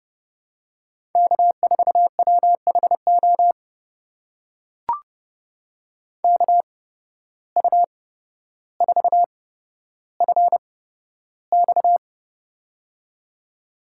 Morse code, heard as a keyed tone, about 30 wpm, spaced as for 13 wpm, 700 Hz. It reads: K4WHO KU4FX
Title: USA Call Signs - 5 Characters - 8x Character Spacing - 30wpm